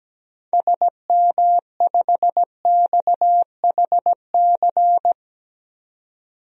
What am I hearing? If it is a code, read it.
Morse code, heard as a keyed tone, about 17 wpm, 700 Hz. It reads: SM5XHC
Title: DXCC-100 Call Signs - 6 Characters - 17wpm